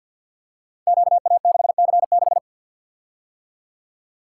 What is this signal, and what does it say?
Morse code, heard as a keyed tone, about 52 wpm, 700 Hz. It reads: QA6C6
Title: Code Groups - 5 Letter:Numbers - 50wpm